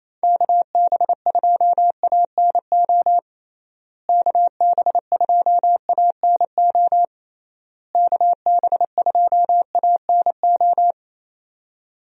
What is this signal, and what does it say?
Morse code, heard as a keyed tone, about 28 wpm, 700 Hz. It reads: KB2ANO KB2ANO KB2ANO
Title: USA Call Signs - 6 Characters - Repeated 3x - 28wpm